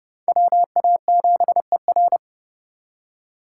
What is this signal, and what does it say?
Morse code, heard as a keyed tone, about 27 wpm, 700 Hz. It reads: WA7ER